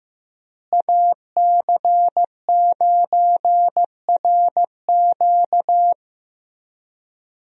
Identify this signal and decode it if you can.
Morse code, heard as a keyed tone, about 15 wpm, 700 Hz. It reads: AC9RQ
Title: USA Call Signs - 5 Characters - Repeated 2x - 15wpm